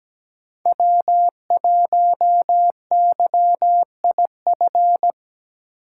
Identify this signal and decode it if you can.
Morse code, heard as a keyed tone, about 17 wpm, 700 Hz. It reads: W1YIF
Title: USA Call Signs - 5 Characters - 8x Character Spacing - 17wpm